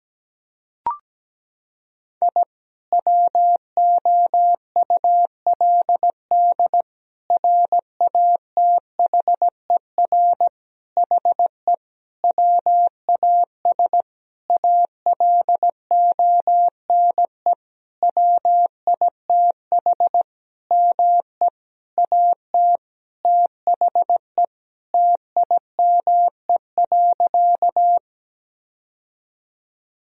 Morse code, heard as a keyed tone, about 17 wpm, 700 Hz. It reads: I WOULD RATHER HE WAS ALONE WITH ME AT THE TIME.
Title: Sentences from Top 1000 Words - Review Part 4 - 17wpm